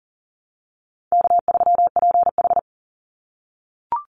Morse code, heard as a keyed tone, about 39 wpm, 700 Hz. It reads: K3PH
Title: Easy Call Signs - 40wpm